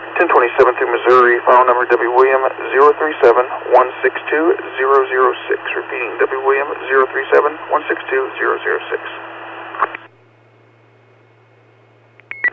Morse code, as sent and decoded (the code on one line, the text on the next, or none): none